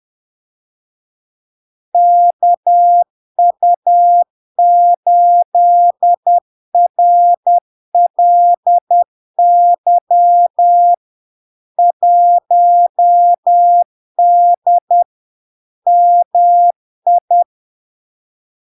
Morse code, heard as a keyed tone, about 10 wpm, 700 Hz. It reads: KU8RLY 1D MI
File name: LARK-FD-2022-Log-in-CW-10wpm-700-Hz